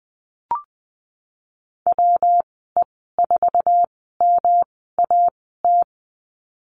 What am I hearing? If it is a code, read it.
Morse code, heard as a keyed tone, about 20 wpm, spaced as for 15 wpm, 700 Hz. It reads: WE4MAT
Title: 193 Lesson 31g - Letter v - Call Signs 20wpm